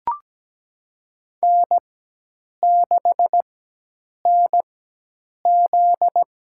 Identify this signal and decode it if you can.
Morse code, heard as a keyed tone, about 17 wpm, spaced as for 8 wpm, 700 Hz. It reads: N6NZ
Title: USA Call Signs - 4x Character Spacing - 17wpm